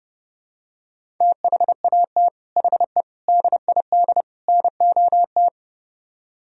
Morse code, heard as a keyed tone, about 30 wpm, 700 Hz. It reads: THAT HE DID NOT